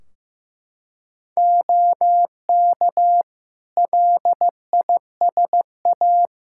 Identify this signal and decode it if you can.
Morse code, heard as a keyed tone, about 15 wpm, 700 Hz. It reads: OK LISA
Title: Practice QSO - Exchange 3 - More Info - 15wpm